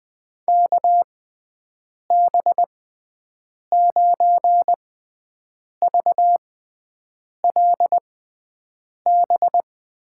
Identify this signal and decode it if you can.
Morse code, heard as a keyed tone, about 20 wpm, spaced as for 7 wpm, 700 Hz. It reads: KB9VLB
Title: USA Call Signs - 6 Characters - 6x Character Spacing - 20wpm